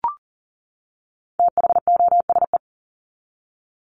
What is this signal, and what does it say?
Morse code, heard as a keyed tone, about 40 wpm, 700 Hz. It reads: THOSE
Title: Top 300 Words 40wpm